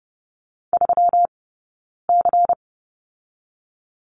Morse code, heard as a keyed tone, about 30 wpm, spaced as for 9 wpm, 700 Hz. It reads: 3C
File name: Lesson 20d - Number 3 - Two Characters 30wpm